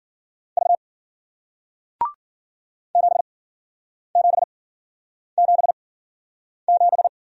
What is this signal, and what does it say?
Morse code, heard as a keyed tone, about 56 wpm, spaced as for 9 wpm, 700 Hz. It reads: V 7777